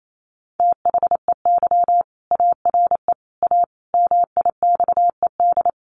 Morse code, heard as a keyed tone, about 28 wpm, 700 Hz. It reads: THEY ARE A MIXED
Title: Sentences from Top 1500 Words 28wpm